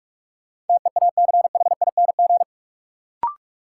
Morse code, heard as a keyed tone, about 42 wpm, 700 Hz. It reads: TEACHING
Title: Sentences from Top 1100 Words - New Word Drill Alt - 45wpm